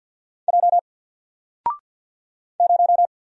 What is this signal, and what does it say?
Morse code, heard as a keyed tone, about 48 wpm, 700 Hz. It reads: J 0
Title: Single Letter-Number Alt 50wpm